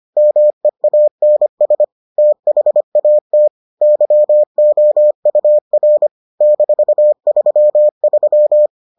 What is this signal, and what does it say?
Morse code, heard as a keyed tone, about 25 wpm, 600 Hz. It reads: MEANS THAT YOUR -33